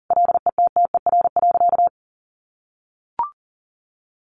Morse code, heard as a keyed tone, about 40 wpm, 700 Hz. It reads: LETTER.
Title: Sentences from Top 1400 Words 40wpm